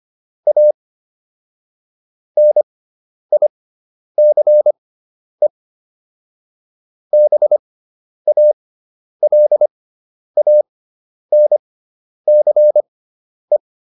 Morse code, heard as a keyed tone, about 25 wpm, spaced as for 10 wpm, 600 Hz. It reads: A NICE BALANCE